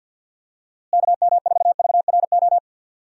Morse code, heard as a keyed tone, about 50 wpm, 700 Hz. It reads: KM4VRO
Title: USA Call Signs - 6 Characters - Repeated 3x - 50wpm